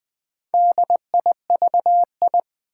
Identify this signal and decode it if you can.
Morse code, heard as a keyed tone, about 20 wpm, 700 Hz. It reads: DIVI
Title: Build A Word - 2 Letters At A Time - 6-Letter Words - 20wpm